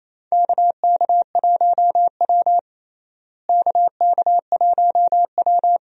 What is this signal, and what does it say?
Morse code, heard as a keyed tone, about 28 wpm, 700 Hz. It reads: KK1W KK1W
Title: USA Call Signs - 4 Characters - Repeated 3x - 28wpm